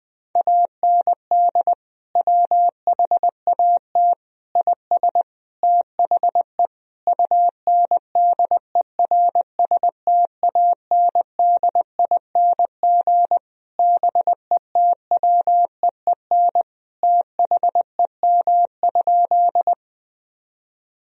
Morse code, heard as a keyed tone, about 20 wpm, 700 Hz. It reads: AND WHAT IS THE UNDERSTANDING BETWEEN THEM?
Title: Sentences from Top 1000 Words - Review Part 3 - 20wpm